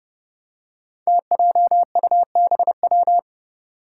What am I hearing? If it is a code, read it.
Morse code, heard as a keyed tone, about 30 wpm, 700 Hz. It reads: TJUBW
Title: Five Challenging Letter-Number Combinations 30wpm